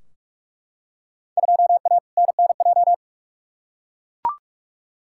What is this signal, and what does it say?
Morse code, heard as a keyed tone, about 43 wpm, 700 Hz. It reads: 2A NNJ
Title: Field Day Exchanges - 45wpm